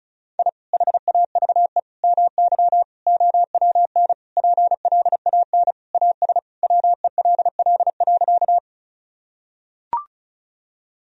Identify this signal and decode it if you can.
Morse code, heard as a keyed tone, about 36 wpm, 700 Hz. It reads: I HAVE MY OWN PLAN AS WELL.
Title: Sentences from Top 1000 Words - Review Part 2 - 35wpm